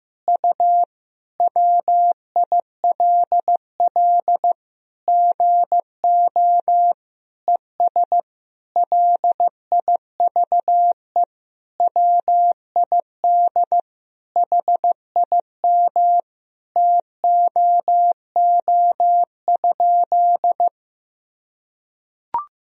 Morse code, heard as a keyed tone, about 15 wpm, 700 Hz. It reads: U WILL GO ES LIVE WID HIM TOO?
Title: Sentences from Top 500 Words and Abbreviations 15wpm